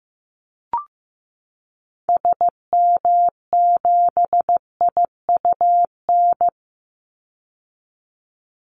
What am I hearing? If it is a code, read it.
Morse code, heard as a keyed tone, about 15 wpm, 700 Hz. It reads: SM7IUN
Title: Call Signs Repeated 3x - 15wpm